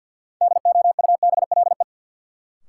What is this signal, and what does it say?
Morse code, heard as a keyed tone, about 47 wpm, 700 Hz. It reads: DOUBLE
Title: Sentences from Top 1400 Words - New Word Drill Alt - 50wpm